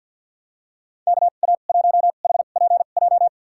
Morse code, heard as a keyed tone, about 47 wpm, spaced as for 35 wpm, 700 Hz. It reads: KA1HPJ
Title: USA Call Signs - 2x Character Spacing - 50wpm